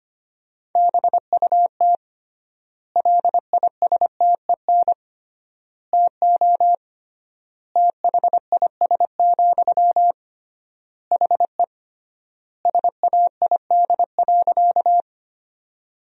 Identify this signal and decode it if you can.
Morse code, heard as a keyed tone, about 25 wpm, 700 Hz. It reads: BUT LISTEN TO THIS, HE SAID.
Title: 3x Word-Spacing - Sentences from Top 400 Words - 25wpm